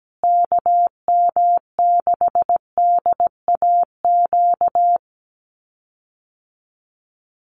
Morse code, heard as a keyed tone, about 17 wpm, 700 Hz. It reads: KM6DAQ
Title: USA Call Signs Repeated 3x 17wpm